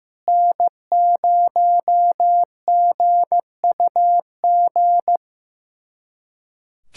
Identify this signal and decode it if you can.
Morse code, heard as a keyed tone, about 15 wpm, 700 Hz. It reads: N0GUG
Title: USA Callsign DE Callsign - 5 Characters - 15wpm